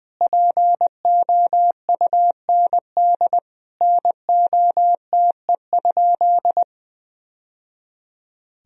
Morse code, heard as a keyed tone, about 20 wpm, 700 Hz. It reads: POUND NOTE?
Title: Sentences from Top 1000 Words - Review Part 2 - 20wpm